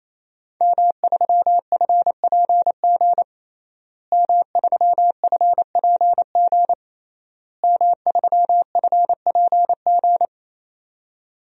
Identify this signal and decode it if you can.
Morse code, heard as a keyed tone, about 28 wpm, 700 Hz. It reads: M3FPG M3FPG M3FPG
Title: DXCC-100 Call Signs - 5 Characters - Repeated 3x - 28wpm